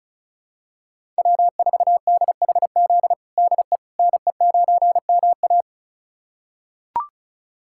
Morse code, heard as a keyed tone, about 36 wpm, 700 Hz. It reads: W4DHZ DE NE9MA
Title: USA Callsign DE Callsign - 5 Characters - 35wpm